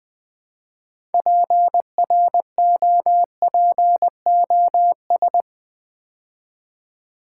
Morse code, heard as a keyed tone, about 20 wpm, 700 Hz. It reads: PROPOS
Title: Build A Word - 3 Letters At A Time - 8-Letter Words - 20wpm